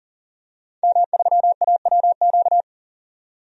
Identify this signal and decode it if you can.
Morse code, heard as a keyed tone, about 40 wpm, 700 Hz. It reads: M3AWQ
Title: DXCC-100 Call Signs - 5 Characters - 40wpm